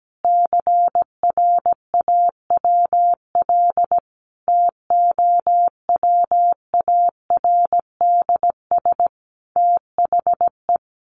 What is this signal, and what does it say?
Morse code, heard as a keyed tone, about 17 wpm, 700 Hz. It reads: CRAWL TOWARDS THE
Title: Additional Easy Phrases 17wpm